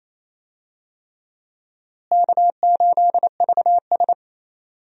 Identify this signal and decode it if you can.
Morse code, heard as a keyed tone, about 27 wpm, 700 Hz. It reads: K8VS
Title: USA Call Signs - 4 Characters - Repeated 3x - 28wpm